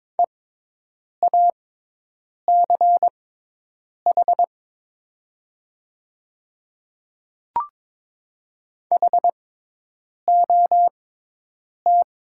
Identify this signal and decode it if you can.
Morse code, heard as a keyed tone, about 22 wpm, spaced as for 8 wpm, 700 Hz. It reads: EACH HOT
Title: 6x Character Spacing - Top 100 Words - 22wpm